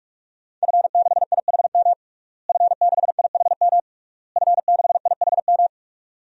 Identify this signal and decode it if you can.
Morse code, heard as a keyed tone, about 46 wpm, 700 Hz. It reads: F6IHM F6IHM F6IHM
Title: DXCC-100 Call Signs - 5 Characters - Repeated 3x - 45wpm